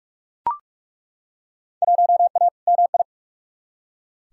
Morse code, heard as a keyed tone, about 42 wpm, 700 Hz. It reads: 1A MI
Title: Field Day Exchanges - 45wpm